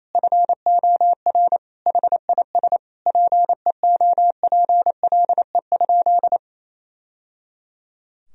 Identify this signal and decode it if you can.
Morse code, heard as a keyed tone, about 28 wpm, 700 Hz. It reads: FOR HIS PEOPLE?
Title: Sentences from Top 1000 Words - Review Part 3 - 28wpm